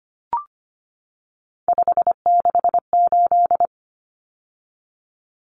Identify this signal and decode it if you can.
Morse code, heard as a keyed tone, about 25 wpm, 700 Hz. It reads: 568
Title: Numbers - 3-Digit 25wpm